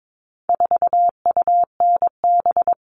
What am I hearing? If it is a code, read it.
Morse code, heard as a keyed tone, about 22 wpm, 700 Hz. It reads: 4UNB
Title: All Cut Numbers - 4-Digit 22wpm